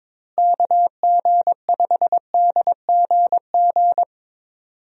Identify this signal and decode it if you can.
Morse code, heard as a keyed tone, about 22 wpm, 700 Hz. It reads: KG5DGG
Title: USA Call Signs - 6 Characters - Repeated 3x - 22wpm